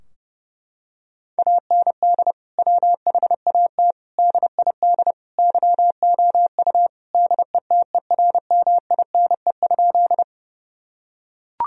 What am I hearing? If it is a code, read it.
Morse code, heard as a keyed tone, about 30 wpm, 700 Hz. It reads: AND WHAT DID YOU DETERMINE?
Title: Sentences from Top 1000 Words - Review Part 4 - 30wpm